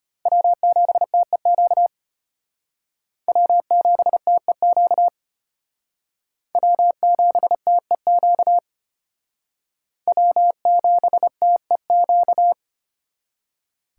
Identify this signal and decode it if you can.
Morse code, heard as a keyed tone, about 31 wpm, 700 Hz. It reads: W7TEQ W7TEQ W7TEQ W7TEQ
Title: Speed Racing - USA 1x3 Call Signs 25wpm